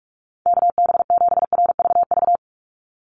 Morse code, heard as a keyed tone, about 44 wpm, 700 Hz. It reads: KB7RVV
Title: USA Call Signs 45wpm